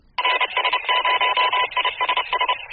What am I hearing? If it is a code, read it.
Morse code, heard as a keyed tone, about 30 wpm, 1000 Hz, with noise. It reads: RS0ISS
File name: ISS CW RS0ISS K